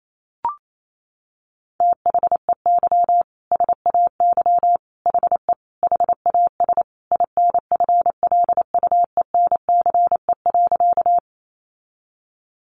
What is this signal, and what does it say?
Morse code, heard as a keyed tone, about 28 wpm, 700 Hz. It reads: THEY SAY HE HAS INFLUENCE.
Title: Sentences from Top 1100 Words 28wpm